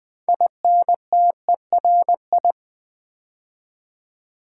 Morse code, heard as a keyed tone, about 20 wpm, 700 Hz. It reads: INTERI
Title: Build A Word - 2 Letters At A Time - 8-Letter Words - 20wpm